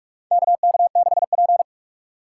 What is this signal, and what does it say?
Morse code, heard as a keyed tone, about 47 wpm, 700 Hz. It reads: KK6P
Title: USA Call Signs - 4 Characters - 2x Character Spacing - 45wpm